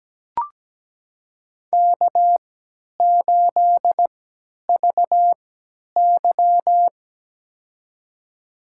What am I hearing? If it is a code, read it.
Morse code, heard as a keyed tone, about 17 wpm, spaced as for 10 wpm, 700 Hz. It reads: K8VY
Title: USA Call Signs - 4 Characters - 3x Character Spacing - 17wpm